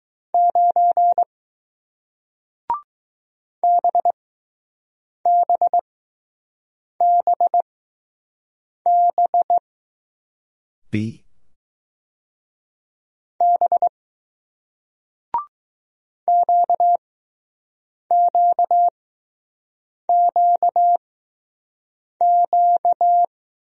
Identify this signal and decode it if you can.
Morse code, heard as a keyed tone, about 20 wpm, spaced as for 6 wpm, 700 Hz. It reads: 9 BBBB B QQQQ